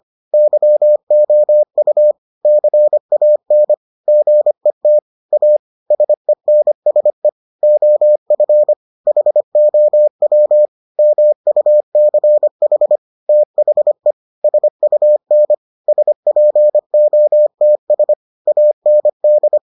Morse code, heard as a keyed tone, about 25 wpm, 600 Hz. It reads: YOU CAN GET A SENSE OF HOW MUCH THE SUN SPOTS AND